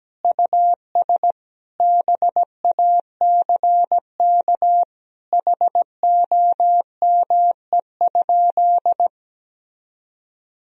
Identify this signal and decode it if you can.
Morse code, heard as a keyed tone, about 17 wpm, 700 Hz. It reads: US BACK HOME?
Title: Sentences from Top 1000 Words - Review Part 2 - 17wpm